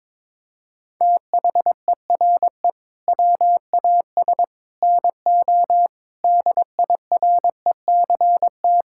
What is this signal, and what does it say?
Morse code, heard as a keyed tone, about 22 wpm, 700 Hz. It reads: THERE WAS NO DIRECT
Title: Sentences from Top 1000 Words - Review Part 6 - 22wpm